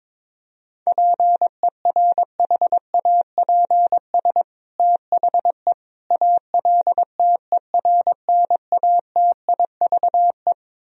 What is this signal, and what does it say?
Morse code, heard as a keyed tone, about 21 wpm, 700 Hz. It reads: PERHAPS THE ALTERNATIVE
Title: Sentences from Top 1000 Words - Review Part 4 - 22wpm